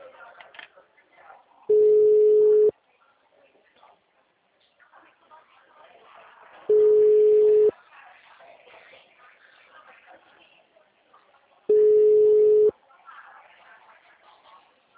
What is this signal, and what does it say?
Ringback tone (Europe, 425 Hz), with speech.
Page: Araras (FGTS)